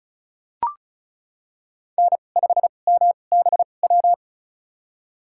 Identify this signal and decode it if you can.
Morse code, heard as a keyed tone, about 36 wpm, spaced as for 25 wpm, 700 Hz. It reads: N5MBW
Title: USA Call Signs - 5 Characters - 2x Character Spacing - 35wpm